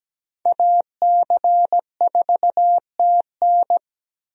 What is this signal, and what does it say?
Morse code, heard as a keyed tone, about 17 wpm, 700 Hz. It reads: AC4TN
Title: USA Call Signs - 5 Characters - Repeated 3x - 17wpm